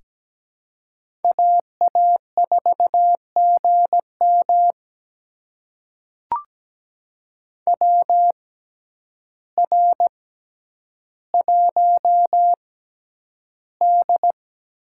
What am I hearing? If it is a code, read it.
Morse code, heard as a keyed tone, about 17 wpm, spaced as for 9 wpm, 700 Hz. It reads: AA4GM WR1D
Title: USA Call Signs - 6x Character Spacing - 17wpm